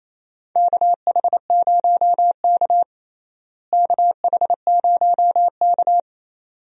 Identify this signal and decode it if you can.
Morse code, heard as a keyed tone, about 28 wpm, 700 Hz. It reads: KH0K KH0K
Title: USA Call Signs - 4 Characters - Repeated 2x - 28wpm